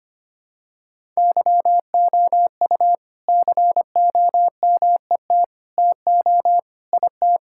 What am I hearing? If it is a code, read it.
Morse code, heard as a keyed tone, about 25 wpm, 700 Hz. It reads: YOU COMET TO IT
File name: practice / 25wpm / All-Age Appropriate Jokes - 25wpm